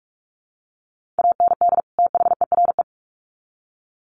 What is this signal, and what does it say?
Morse code, heard as a keyed tone, about 44 wpm, 700 Hz. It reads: AND THERE
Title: Sets of 2 Words 45wpm